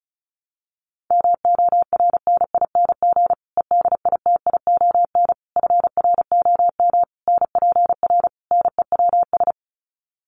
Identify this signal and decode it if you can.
Morse code, heard as a keyed tone, about 34 wpm, 700 Hz. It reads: MORNING EDITION FROM NPR NEWS
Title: Sets of 5 Words Encore 35wpm